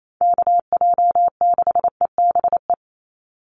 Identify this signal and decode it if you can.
Morse code, heard as a keyed tone, about 28 wpm, 700 Hz. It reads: KJ6EBE
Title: USA Call Signs 28wpm